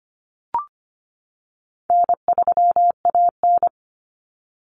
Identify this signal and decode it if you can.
Morse code, heard as a keyed tone, about 25 wpm, 700 Hz. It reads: N3AN
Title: Common Cut Numbers - 4-Digit 25wpm